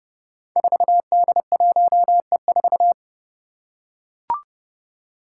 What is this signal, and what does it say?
Morse code, heard as a keyed tone, about 29 wpm, 700 Hz. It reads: 4D1E4
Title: Five Challenging Letter-Number Combinations 30wpm